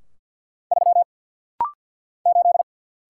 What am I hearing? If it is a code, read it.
Morse code, heard as a keyed tone, about 50 wpm, 700 Hz. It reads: 3 8
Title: Single Letter-Number - Rapid-Fire 50wpm